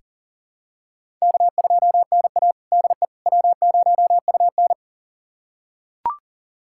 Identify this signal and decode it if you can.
Morse code, heard as a keyed tone, about 40 wpm, 700 Hz. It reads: K2NA DE W0UN
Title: USA Callsign DE Callsign - 4 Characters - 40wpm